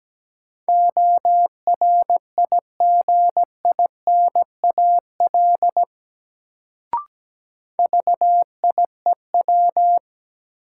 Morse code, heard as a keyed tone, about 17 wpm, 700 Hz. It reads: ORIGINAL VIEW
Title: Sentences from Top 900 Words - New Word Drill - 17wpm